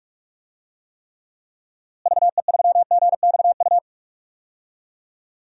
Morse code, heard as a keyed tone, about 44 wpm, 700 Hz. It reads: VE3GXU
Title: DXCC-100 Call Signs - 6 Characters - Repeated 2x - 45wpm